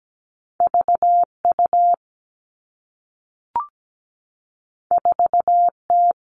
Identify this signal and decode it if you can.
Morse code, heard as a keyed tone, about 17 wpm, 700 Hz. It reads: VU 4T
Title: Two Challenging Letter-Number Combinations 17wpm